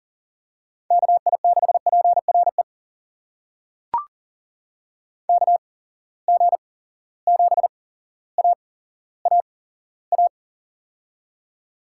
Morse code, heard as a keyed tone, about 42 wpm, spaced as for 17 wpm, 700 Hz. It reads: KI6PRE KG7AAA
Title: USA Call Signs - 6 Characters - 8x Character Spacing - 40wpm